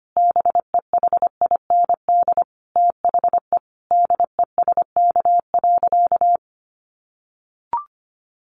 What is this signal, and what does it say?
Morse code, heard as a keyed tone, about 25 wpm, 700 Hz. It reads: BEHIND THE DESK.
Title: A Sound Of Thunder 25wpm